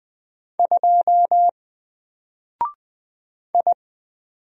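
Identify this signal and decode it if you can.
Morse code, heard as a keyed tone, about 20 wpm, 700 Hz. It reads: 2 I
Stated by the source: Single letter and number Morse drill at 20 wpm, from N4LCD.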